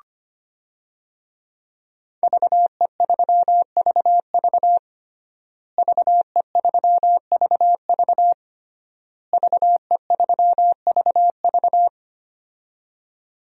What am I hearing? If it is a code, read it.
Morse code, heard as a keyed tone, about 25 wpm, 700 Hz. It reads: VE3VV VE3VV VE3VV
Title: DXCC-100 Call Signs - 5 Characters - Repeated 3x - 25wpm